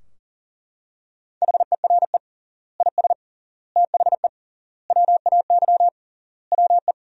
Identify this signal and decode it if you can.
Morse code, heard as a keyed tone, about 37 wpm, 700 Hz. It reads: HERE IS THE WAY WE
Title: 3x Word-Spacing - Sentences from Top 200 Words - 40wpm